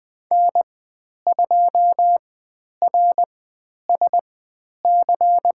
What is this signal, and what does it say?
Morse code, heard as a keyed tone, about 20 wpm, spaced as for 10 wpm, 700 Hz. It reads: N2RSC
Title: 109 Lesson 18g - Letter m - Call Signs 20wpm